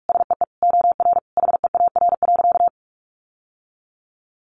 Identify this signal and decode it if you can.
Morse code, heard as a keyed tone, about 44 wpm, 700 Hz. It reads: SEE OR HEAR.
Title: Sentences from Top 600 Words 45wpm